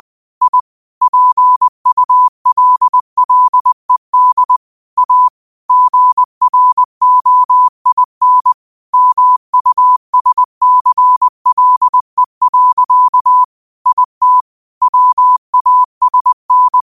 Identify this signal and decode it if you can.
Morse code, heard as a keyed tone, about 20 wpm, 1000 Hz. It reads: I PULLED A GROIN MUSCLE. IT WASN
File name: qotdfiles / 20wpm